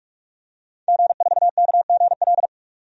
Morse code, heard as a keyed tone, about 46 wpm, 700 Hz. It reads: G4KGL
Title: DXCC-100 Call Signs - 5 Characters - 45wpm